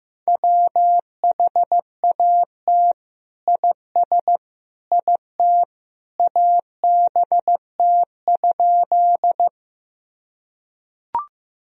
Morse code, heard as a keyed tone, about 16 wpm, 700 Hz. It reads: WHAT IS IT ABT?
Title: Sentences from Top 100 Words and Abbreviations 15wpm